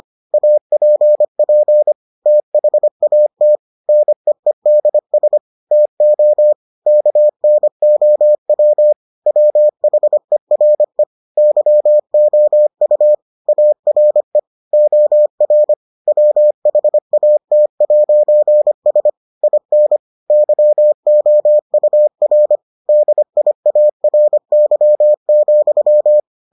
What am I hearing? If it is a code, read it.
Morse code, heard as a keyed tone, about 25 wpm, 600 Hz. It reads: APP THAT NEEDS TO KNOW WHERE YOU ARE OR WHAT'S IN YOUR DIARY,